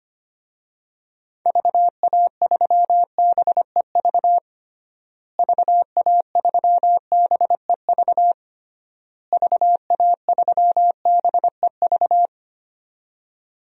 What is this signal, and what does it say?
Morse code, heard as a keyed tone, about 25 wpm, 700 Hz. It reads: VA3BEV VA3BEV VA3BEV
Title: DXCC-100 Call Signs - 6 Characters - Repeated 3x - 25wpm